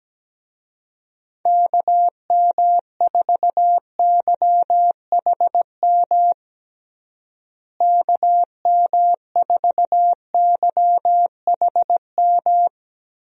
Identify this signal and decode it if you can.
Morse code, heard as a keyed tone, about 17 wpm, 700 Hz. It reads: KM4YHM KM4YHM
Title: USA Call Signs - 6 Characters - Repeated 3x - 17wpm